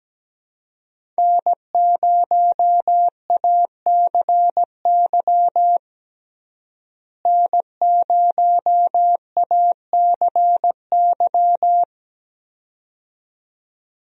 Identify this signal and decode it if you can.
Morse code, heard as a keyed tone, about 17 wpm, 700 Hz. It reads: N0ACY N0ACY
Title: USA Call Signs - 5 Characters - Repeated 2x - 17wpm